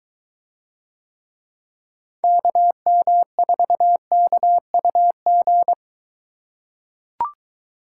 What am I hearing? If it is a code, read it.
Morse code, heard as a keyed tone, about 23 wpm, 700 Hz. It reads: KM4KUG